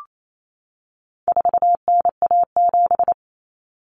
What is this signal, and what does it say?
Morse code, heard as a keyed tone, about 28 wpm, 700 Hz. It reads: 4NA7
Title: Common Cut Numbers - 4-Digit 28wpm